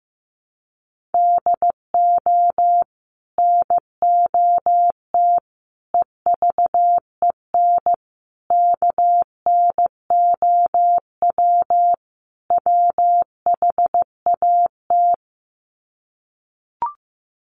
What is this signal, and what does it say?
Morse code, heard as a keyed tone, about 15 wpm, 700 Hz. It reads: DO NOT EVEN KNOW WHAT
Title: Sets of 5 Words Encore 15wpm